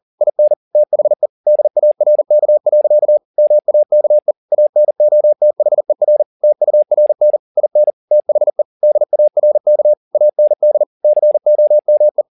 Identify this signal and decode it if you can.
Morse code, heard as a keyed tone, about 40 wpm, 600 Hz. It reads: IN THE DARK. MAKE ANOTHER TURN IN THE DARK AND COME